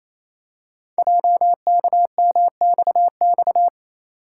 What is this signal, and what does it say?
Morse code, heard as a keyed tone, about 29 wpm, 700 Hz. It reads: JKMXX
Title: Code Groups - 5 Letters - 28wpm